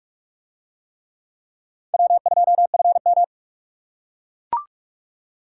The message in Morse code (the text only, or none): W2FM